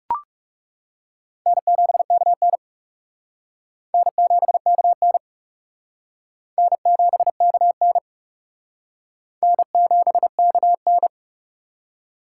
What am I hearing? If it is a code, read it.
Morse code, heard as a keyed tone, about 35 wpm, 700 Hz. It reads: N7KN N7KN N7KN N7KN
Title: Speed Racing - USA 1x2 Call Signs 30wpm